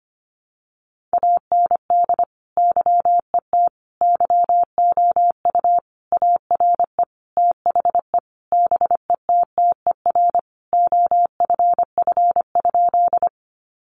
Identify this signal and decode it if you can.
Morse code, heard as a keyed tone, about 25 wpm, 700 Hz. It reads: AND YET YOU ARE THE BETTER OFF?
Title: Sentences from Top 500 Words 25wpm